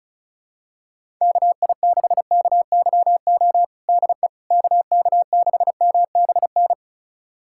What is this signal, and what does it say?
Morse code, heard as a keyed tone, about 34 wpm, 700 Hz. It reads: KI6KYO DE KK6MBN
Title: USA Callsign DE Callsign - 6 Characters - 35wpm